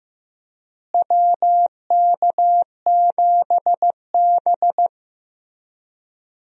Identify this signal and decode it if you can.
Morse code, heard as a keyed tone, about 15 wpm, 700 Hz. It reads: WK7B
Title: USA Call Signs - 4 Characters - Repeated 3x - 15wpm